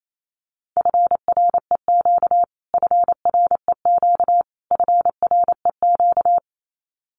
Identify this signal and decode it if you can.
Morse code, heard as a keyed tone, about 28 wpm, 700 Hz. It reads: FREQ FREQ FREQ
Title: Abbreviations Repeated 3x - 28wpm